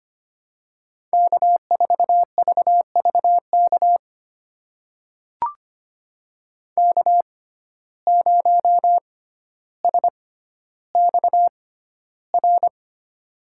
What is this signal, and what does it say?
Morse code, heard as a keyed tone, about 25 wpm, spaced as for 13 wpm, 700 Hz. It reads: K4VVK K0SXR
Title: USA Call Signs - 5 Characters - 6x Character Spacing - 25wpm